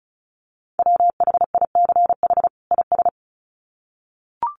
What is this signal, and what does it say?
Morse code, heard as a keyed tone, about 34 wpm, 700 Hz. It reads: WHICH IS
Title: Sets of 2 Words 35wpm